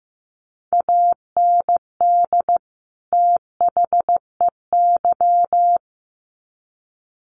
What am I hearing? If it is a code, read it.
Morse code, heard as a keyed tone, about 15 wpm, 700 Hz. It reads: AND THEY
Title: Sets of 2 Words 15wpm